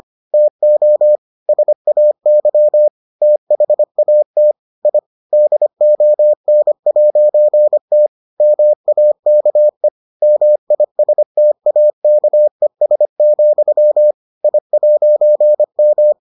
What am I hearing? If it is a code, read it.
Morse code, heard as a keyed tone, about 25 wpm, 600 Hz. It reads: TO SAY THAT I DON'T MAKE MISTAKES, I'M